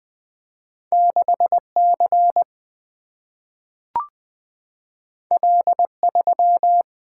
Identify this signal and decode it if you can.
Morse code, heard as a keyed tone, about 20 wpm, 700 Hz. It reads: 6C L3
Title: Single-Letter Single-Number 20wpm